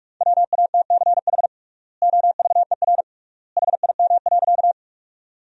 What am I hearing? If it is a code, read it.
Morse code, heard as a keyed tone, about 45 wpm, 700 Hz. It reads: WATCH OVER HIM.
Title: 3x Word-Spacing - Sentences from Top 300 Words - 45wpm